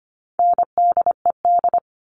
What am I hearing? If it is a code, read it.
Morse code, heard as a keyed tone, about 25 wpm, 700 Hz. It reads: NDED
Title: All Cut Numbers - 4-Digit 25wpm